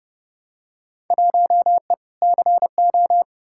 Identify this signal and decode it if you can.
Morse code, heard as a keyed tone, about 32 wpm, 700 Hz. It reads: 1E CO
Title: Field Day Exchanges - 30wpm